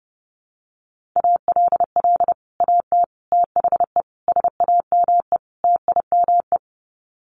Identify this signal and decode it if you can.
Morse code, heard as a keyed tone, about 30 wpm, 700 Hz. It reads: ALL AT THE SAME TIME
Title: Sets of 5 Words Encore2 30wpm